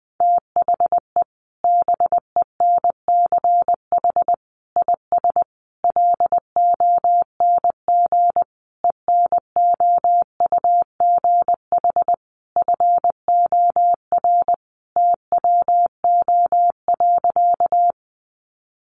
Morse code, heard as a keyed tone, about 20 wpm, 700 Hz. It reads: THE BENCH IS LONG ENOUGH FOR TWO.
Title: Sentences from Top 1400 Words 20wpm